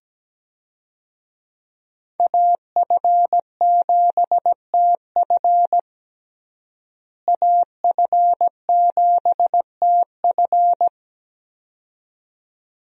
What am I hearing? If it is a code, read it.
Morse code, heard as a keyed tone, about 17 wpm, 700 Hz. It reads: AF7TF AF7TF
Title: USA Call Signs - 5 Characters - Repeated 2x - 17wpm